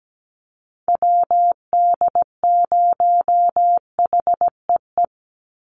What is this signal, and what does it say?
Morse code, heard as a keyed tone, about 17 wpm, 700 Hz. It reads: WD0HEE
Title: USA Call Signs 17wpm